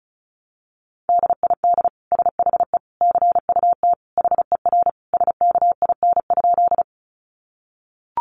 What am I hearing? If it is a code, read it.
Morse code, heard as a keyed tone, about 34 wpm, 700 Hz. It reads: DID SHE CUT HER SKIN?
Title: Sentences from Top 1300 Words 35wpm